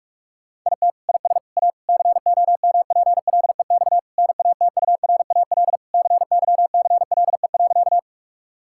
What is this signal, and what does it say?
Morse code, heard as a keyed tone, about 46 wpm, 700 Hz. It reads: IT IS A COMPLEX NATURAL CYCLE.
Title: Sentences from Top 1000 Words - Review Part 6 - 45wpm